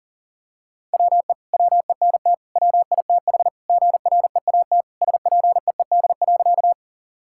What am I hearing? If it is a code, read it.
Morse code, heard as a keyed tone, about 40 wpm, 700 Hz. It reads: WE WENT WITH GREAT SPEED.
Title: Sentences from Top 1000 Words - Review Part 5 - 40wpm